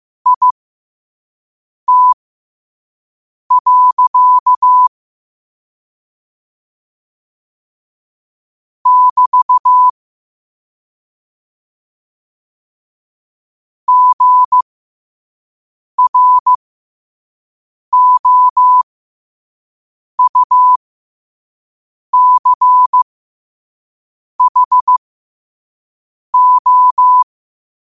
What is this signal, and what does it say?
Morse code, heard as a keyed tone, about 15 wpm, spaced as for 5 wpm, 1000 Hz. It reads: IT. <BT> GROUCHO
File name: qotdfiles / 5wpm